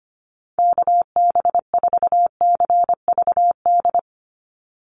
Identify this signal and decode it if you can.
Morse code, heard as a keyed tone, about 25 wpm, 700 Hz. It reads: KB4CVD
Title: USA Call Signs 25wpm